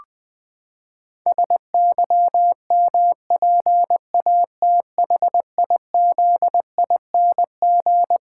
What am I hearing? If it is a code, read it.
Morse code, heard as a keyed tone, about 20 wpm, 700 Hz. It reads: SYMPATHIZING
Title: Pangram Challenge v2 - 20wpm